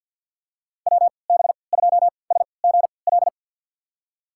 Morse code, heard as a keyed tone, about 47 wpm, spaced as for 27 wpm, 700 Hz. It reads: WB2SGL
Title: USA Call Signs - 6 Characters - 3x Character Spacing - 50wpm